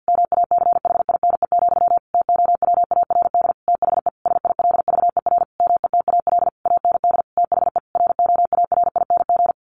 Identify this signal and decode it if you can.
Morse code, heard as a keyed tone, about 50 wpm, 700 Hz. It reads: MACHINE, TOWARD THE SILVER METAL AND THE ROARING
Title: A Sound Of Thunder 50wpm